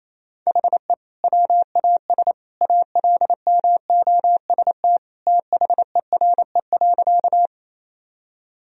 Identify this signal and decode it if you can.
Morse code, heard as a keyed tone, about 28 wpm, 700 Hz. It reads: HE WAS ALMOST THERE.
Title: Sentences from Top 1000 Words - Review Part 4 - 28wpm